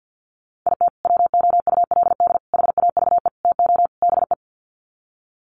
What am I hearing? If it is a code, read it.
Morse code, heard as a keyed tone, about 48 wpm, 700 Hz. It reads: IT WOULD HAVE TO BE
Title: Sets of 5 Words Encore 50wpm